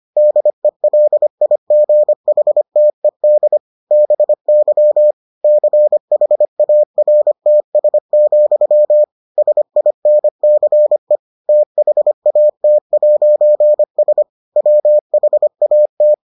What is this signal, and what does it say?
Morse code, heard as a keyed tone, about 25 wpm, 600 Hz. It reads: DELIGHTED BY CHARTS, SINCE THAT'S WHAT